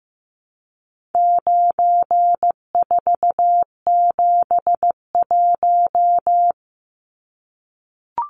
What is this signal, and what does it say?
Morse code, heard as a keyed tone, about 15 wpm, 700 Hz. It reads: 9471
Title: Numbers - 4-Digit 15wpm